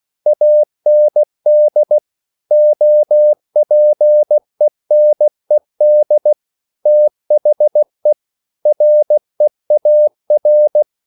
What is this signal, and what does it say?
Morse code, heard as a keyed tone, about 16 wpm, 600 Hz. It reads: AND OPENED THE REAR